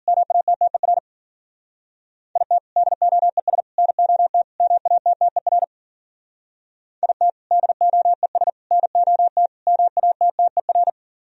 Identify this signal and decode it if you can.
Morse code, heard as a keyed tone, about 50 wpm, 700 Hz. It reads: MATTER IT DOES NOT MATTER IT DOES NOT MATTER